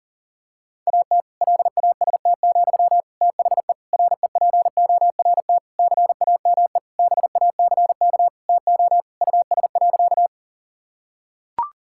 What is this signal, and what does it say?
Morse code, heard as a keyed tone, about 40 wpm, 700 Hz. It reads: AT LAST, THE REPORT CAME BACK TO US.
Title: Sentences from Top 1000 Words - Review Part 6 - 40wpm